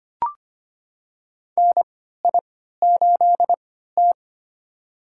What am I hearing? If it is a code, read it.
Morse code, heard as a keyed tone, about 25 wpm, spaced as for 14 wpm, 700 Hz. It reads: NI8T
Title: USA Call Signs - 4 Characters - 3x Character Spacing - 25wpm